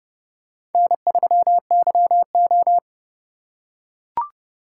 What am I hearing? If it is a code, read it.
Morse code, heard as a keyed tone, about 29 wpm, 700 Hz. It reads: N3YO